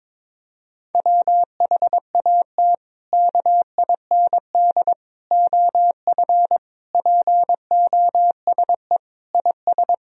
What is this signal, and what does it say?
Morse code, heard as a keyed tone, about 22 wpm, 700 Hz. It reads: WHAT KIND OF POSE IS
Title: Sentences from Top 1000 Words - Review Part 2 - 22wpm